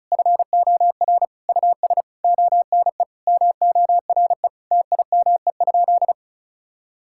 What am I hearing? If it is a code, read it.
Morse code, heard as a keyed tone, about 36 wpm, 700 Hz. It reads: FOR US ONE MORE TIME?
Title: Sentences from Top 1000 Words - Review Part 5 - 35wpm